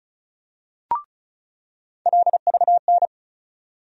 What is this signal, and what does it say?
Morse code, heard as a keyed tone, about 36 wpm, 700 Hz. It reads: LVN
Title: USA Call Sign Postfixes - 3 Letters 35wpm